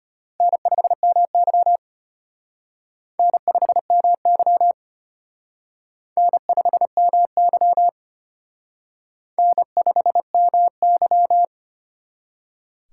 Morse code, heard as a keyed tone, about 32 wpm, 700 Hz. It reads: N5MY N5MY N5MY N5MY